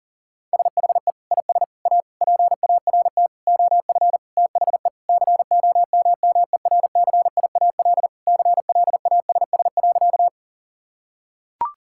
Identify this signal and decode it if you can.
Morse code, heard as a keyed tone, about 40 wpm, 700 Hz. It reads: SHE IS A PART OF THE COMMERCIAL CLASS.
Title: Sentences from Top 1000 Words - Review Part 5 - 40wpm